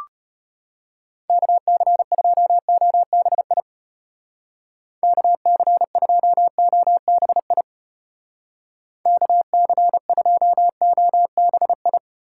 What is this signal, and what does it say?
Morse code, heard as a keyed tone, about 31 wpm, 700 Hz. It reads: KC2OBI KC2OBI KC2OBI
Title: Speed Racing - USA 2x3 Call Signs 25wpm